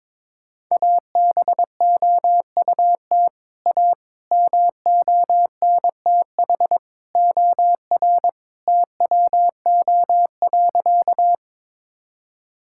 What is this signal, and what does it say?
Morse code, heard as a keyed tone, about 22 wpm, 700 Hz. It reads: ABOUT A MONTH OR TWO.
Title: Sentences from Top 1000 Words - Review Part 5 - 22wpm